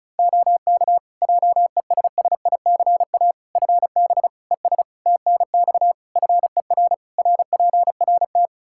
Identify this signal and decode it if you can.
Morse code, heard as a keyed tone, about 36 wpm, 700 Hz. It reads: OK JESSICA FB ES TNX FER RPRT
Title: Practice QSO - Exchange 2 - Info - 35wpm